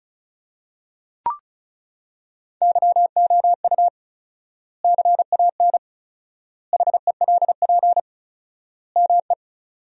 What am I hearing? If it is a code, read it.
Morse code, heard as a keyed tone, about 34 wpm, 700 Hz. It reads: YOU CAN HELP ME
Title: 4x Word-Spacing - Sentences from Top 200 Words - 35wpm